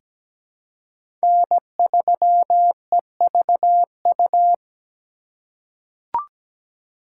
Morse code, heard as a keyed tone, about 17 wpm, 700 Hz. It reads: N3EVU
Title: USA Call Signs - 5 Characters - 8x Character Spacing - 17wpm